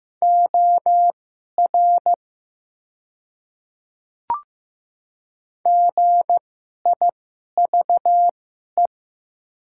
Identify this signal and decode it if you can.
Morse code, heard as a keyed tone, about 15 wpm, spaced as for 11 wpm, 700 Hz. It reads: OR GIVE
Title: 2x Character Spacing - Top 200 Words - 15wpm